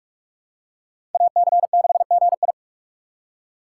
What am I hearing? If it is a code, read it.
Morse code, heard as a keyed tone, about 46 wpm, 700 Hz. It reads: AC6GI
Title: USA Call Signs - 4x Character Spacing - 45wpm